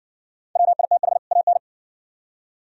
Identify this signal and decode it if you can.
Morse code, heard as a keyed tone, about 68 wpm, 690 Hz. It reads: WITH AN